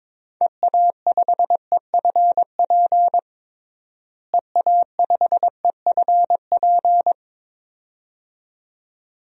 Morse code, heard as a keyed tone, about 22 wpm, 700 Hz. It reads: EA5EFP EA5EFP
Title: DXCC-100 Call Signs - 6 Characters - Repeated 2x - 22wpm